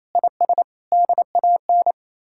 Morse code, heard as a keyed tone, about 37 wpm, 700 Hz. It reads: IS DAN